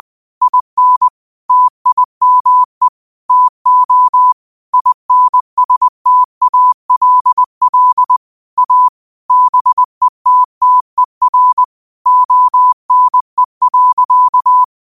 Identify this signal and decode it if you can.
Morse code, heard as a keyed tone, about 20 wpm, 1000 Hz. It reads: IN TIME TO INSTALL A BETTER ONE.
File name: qotdfiles / 20wpm